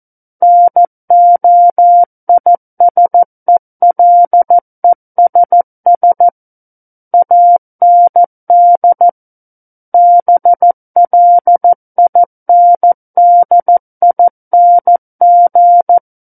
Morse code, heard as a keyed tone, about 14 wpm, 700 Hz. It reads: NOISELESS AND BLINDING
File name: Book One / 05-Chapter 5 - 14 WPM